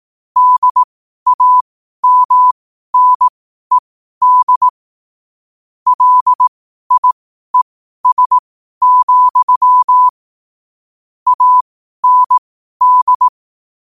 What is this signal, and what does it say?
Morse code, heard as a keyed tone, about 18 wpm, spaced as for 12 wpm, 1000 Hz. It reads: DAMNED LIES, AND
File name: qotdfiles / 12wpm